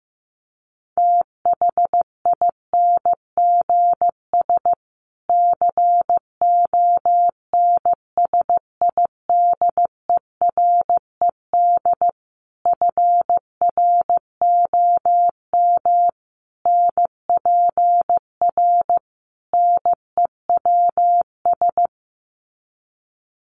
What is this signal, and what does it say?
Morse code, heard as a keyed tone, about 15 wpm, 700 Hz. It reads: THINGS CONSIDERED FROM NPR NEWS
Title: Sets of 5 Words Encore 15wpm